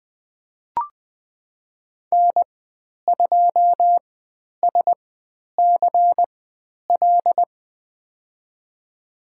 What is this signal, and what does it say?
Morse code, heard as a keyed tone, about 20 wpm, spaced as for 10 wpm, 700 Hz. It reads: N2SCL
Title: 095 Lesson 16g - Letter c - Call Signs 20wpm